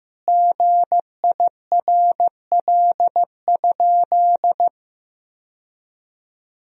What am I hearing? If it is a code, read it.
Morse code, heard as a keyed tone, about 15 wpm, 700 Hz. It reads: GIRL?
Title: Sentences from Top 1000 Words - Review Part 1 - 15wpm